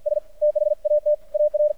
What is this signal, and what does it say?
Morse code, heard as a keyed tone, about 52 wpm, 600 Hz. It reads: S THAT AR
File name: archived / CRSnet